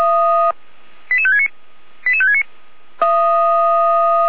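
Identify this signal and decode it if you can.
Five-tone selective call (ZVEI), heard as quick runs of five tones, tones of about 70 ms. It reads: 88569 88569
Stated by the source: ZVEI five-tone selective call, code 88483, received by SDR.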